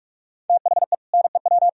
Morse code, heard as a keyed tone, about 49 wpm, 700 Hz. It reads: THE NEW